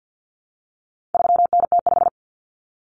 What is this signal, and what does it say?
Morse code, heard as a keyed tone, about 50 wpm, 700 Hz. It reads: 3NT5
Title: Common Cut Numbers - 4-Digit 50wpm